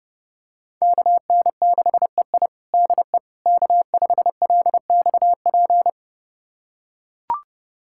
Morse code, heard as a keyed tone, about 30 wpm, 700 Hz. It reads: KN6EI DE K5LXP
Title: USA Callsign DE Callsign - 5 Characters - 30wpm